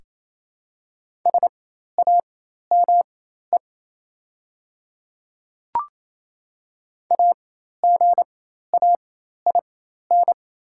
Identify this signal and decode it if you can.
Morse code, heard as a keyed tone, about 28 wpm, spaced as for 13 wpm, 690 Hz. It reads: SAME AGAIN
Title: 4x Character Spacing - Top 200 Words - 28wpm